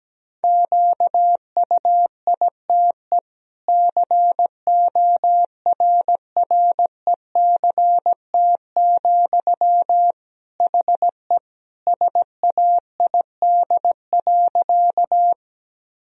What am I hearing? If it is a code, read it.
Morse code, heard as a keyed tone, about 17 wpm, 700 Hz. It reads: QUITE CORRECT, HE SAID.
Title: Sentences from Top 1000 Words - Review Part 2 - 17wpm